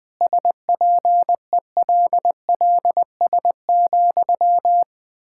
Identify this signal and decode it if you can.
Morse code, heard as a keyed tone, about 20 wpm, 700 Hz. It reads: SPELLS,
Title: Pangram Challenge v2 - 20wpm